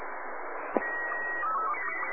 SSTV (slow-scan television), Robot 36.
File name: sat / RADIOSCAF-B 24Dec2011